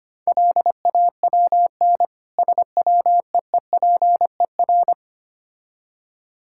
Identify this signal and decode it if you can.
Morse code, heard as a keyed tone, about 25 wpm, 700 Hz. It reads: LAWN SWEEPER
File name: practice / 25wpm / Difficult Words - Speed-Racing with Hints - 25wpm